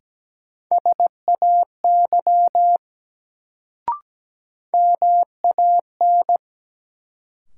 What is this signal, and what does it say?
Morse code, heard as a keyed tone, about 17 wpm, 700 Hz. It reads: SAY MAN
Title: Sentences from Top 200 Words - New Word Drill - 17wpm